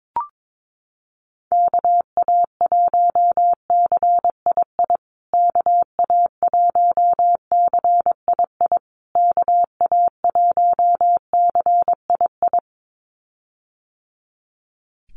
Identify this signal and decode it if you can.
Morse code, heard as a keyed tone, about 22 wpm, 700 Hz. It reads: KA1CII KA1CII KA1CII
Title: USA Call Signs Repeated 3x 22wpm